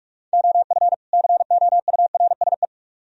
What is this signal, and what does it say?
Morse code, heard as a keyed tone, about 44 wpm, 700 Hz. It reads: OF COURSE